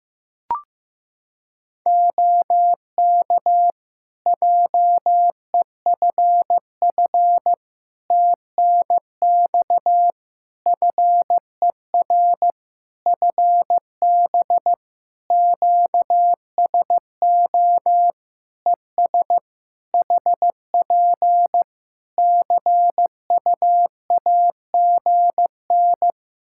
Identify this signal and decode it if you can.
Morse code, heard as a keyed tone, about 15 wpm, 700 Hz. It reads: OK JEFF TNX FER FB QSO ES HP CUAGN
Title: Practice QSO - Exchange 4 - Ending - 15wpm